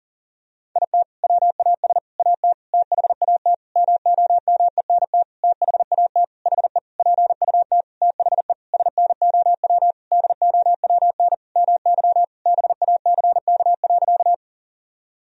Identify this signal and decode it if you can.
Morse code, heard as a keyed tone, about 40 wpm, 700 Hz. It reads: IT WAS AT THAT MOMENT THAT HE PUT THE SNOW DOWN MY BACK.
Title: Sentences from Top 1000 Words - Review Part 3 - 40wpm